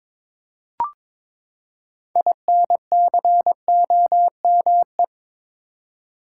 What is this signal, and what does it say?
Morse code, heard as a keyed tone, about 23 wpm, 700 Hz. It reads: INCOME
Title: Six Letter Words - 22wpm